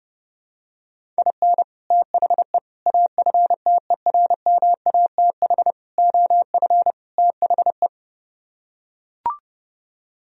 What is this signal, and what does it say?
Morse code, heard as a keyed tone, about 30 wpm, 700 Hz. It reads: IN THE AFTERMATH OF THE